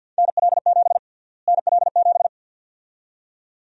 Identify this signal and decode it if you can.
Morse code, heard as a keyed tone, about 52 wpm, 700 Hz. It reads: NL7 NL7
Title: USA Call Sign Prefixes - Repeated 2x - 50wpm